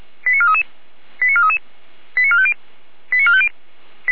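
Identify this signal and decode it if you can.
Five-tone selective call (ZVEI), heard as quick runs of five tones, tones of about 70 ms. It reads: 89433 89433 89450 89450
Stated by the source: ZVEI five-tone selective call received on an SDR.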